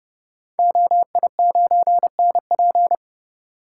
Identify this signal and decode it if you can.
Morse code, heard as a keyed tone, about 31 wpm, 700 Hz. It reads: OI9NP
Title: Code Groups - 5 Letter:Numbers - 30wpm